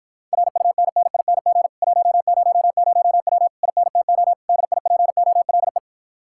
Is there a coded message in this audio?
Yes: Morse code from 0.3 to 5.8 s, heard as a keyed tone, about 54 wpm, 700 Hz. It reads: RUNNING 100W INTO DIPOLE